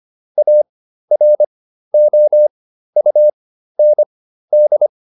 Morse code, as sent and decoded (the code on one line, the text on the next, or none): .- .-. --- ..- -. -..
AROUND